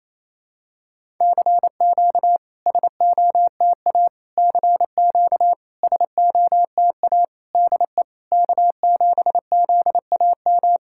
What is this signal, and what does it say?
Morse code, heard as a keyed tone, about 28 wpm, 700 Hz. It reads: CQ SOTA CQ SOTA DE K7ZAM
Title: SOTA QSO Practice 28wpm